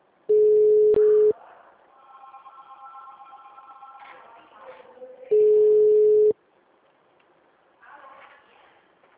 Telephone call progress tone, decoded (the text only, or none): ringback tone